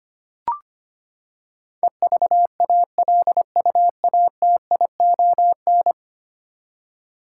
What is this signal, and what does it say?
Morse code, heard as a keyed tone, about 25 wpm, 700 Hz. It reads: EVALUATION
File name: Ten Letter Words - 25wpm